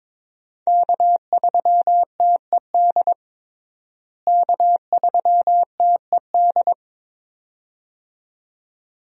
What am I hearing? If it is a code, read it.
Morse code, heard as a keyed tone, about 22 wpm, 700 Hz. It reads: K3TED K3TED
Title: USA Call Signs - 5 Characters - Repeated 2x - 22wpm